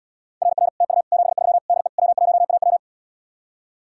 Morse code, heard as a keyed tone, about 80 wpm, 700 Hz. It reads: RR TU 73 DE W1AW